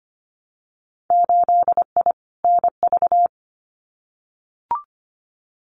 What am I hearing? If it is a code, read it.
Morse code, heard as a keyed tone, about 25 wpm, 700 Hz. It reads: 8I NV
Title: Winter Field Day Exchanges Encore 25wpm